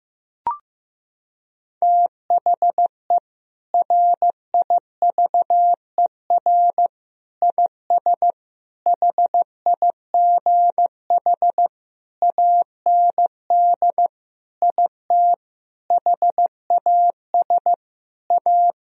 Morse code, heard as a keyed tone, about 15 wpm, 700 Hz. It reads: THE RIVER IS HIGH AND IT HAS A
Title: Sentences from Top 1000 Words - Review Part 4 - 15wpm